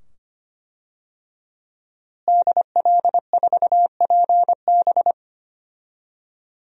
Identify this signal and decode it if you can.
Morse code, heard as a keyed tone, about 25 wpm, 700 Hz. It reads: DL4PB
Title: DXCC-100 Call Signs - 5 Characters - Repeated 2x - 25wpm